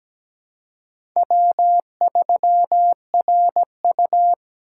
Morse code, heard as a keyed tone, about 17 wpm, 700 Hz. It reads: W3RU
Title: USA Call Signs - 4 Characters - 3x Character Spacing - 17wpm